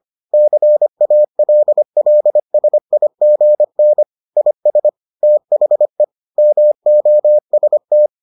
Morse code, heard as a keyed tone, about 25 wpm, 600 Hz. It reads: CALLSIGN IS THE MOST